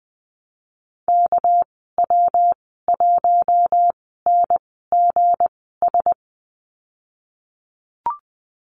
Morse code, heard as a keyed tone, about 20 wpm, spaced as for 15 wpm, 700 Hz. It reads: KW1NGS
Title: 238 Lesson 38g - Letter q - Call Signs 20wpm